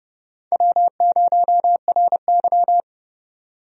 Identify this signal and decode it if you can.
Morse code, heard as a keyed tone, about 32 wpm, 700 Hz. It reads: W0RY
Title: USA Call Signs - 4 Characters - 30wpm